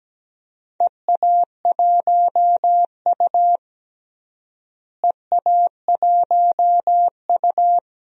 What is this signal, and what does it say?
Morse code, heard as a keyed tone, about 17 wpm, 700 Hz. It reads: EA1U EA1U
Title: DXCC-100 Call Signs - 4 Characters - Repeated 2x - 17wpm